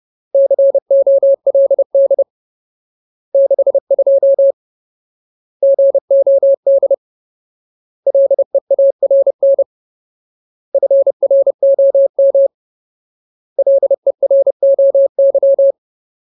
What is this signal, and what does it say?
Morse code, heard as a keyed tone, about 30 wpm, 550 Hz. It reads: COLD 62 GOD LEARN FROM LEROY
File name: Ultimate Word List / 30WPM Ultimate